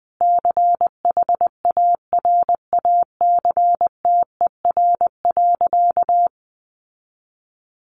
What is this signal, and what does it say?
Morse code, heard as a keyed tone, about 20 wpm, 700 Hz. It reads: CHARACTER.
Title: Sentences from Top 1200 Words 20wpm